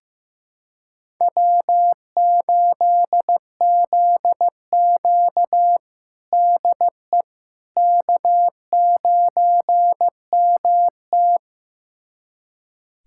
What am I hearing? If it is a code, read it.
Morse code, heard as a keyed tone, about 15 wpm, 700 Hz. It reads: W8ZQ DE K9MT
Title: USA Callsign DE Callsign - 4 Characters - 15wpm